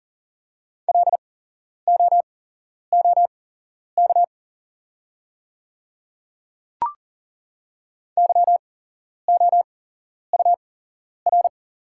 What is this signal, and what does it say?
Morse code, heard as a keyed tone, about 39 wpm, spaced as for 11 wpm, 700 Hz. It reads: LOOK YOUR
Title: 8x Character Spacing - Top 100 Words - 40wpm